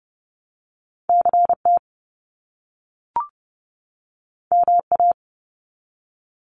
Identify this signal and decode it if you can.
Morse code, heard as a keyed tone, about 30 wpm, 700 Hz. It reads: CT MA
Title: US States 30wpm